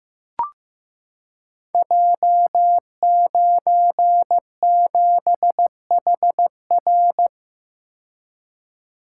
Morse code, heard as a keyed tone, about 15 wpm, 700 Hz. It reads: J97HR
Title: Code Groups - 5 Letter:Numbers - 15wpm